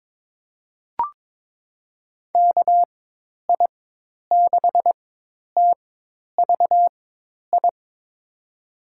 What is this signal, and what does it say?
Morse code, heard as a keyed tone, about 21 wpm, spaced as for 10 wpm, 700 Hz. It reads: KI6TVI
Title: USA Call Signs - 6 Characters - 4x Character Spacing - 22wpm